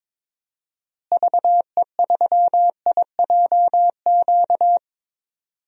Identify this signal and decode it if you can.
Morse code, heard as a keyed tone, about 22 wpm, 700 Hz. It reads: VE3IJQ
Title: DXCC-100 Call Signs - 6 Characters - Repeated 2x - 22wpm